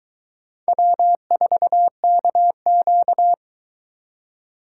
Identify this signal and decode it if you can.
Morse code, heard as a keyed tone, about 22 wpm, 700 Hz. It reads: W4KQ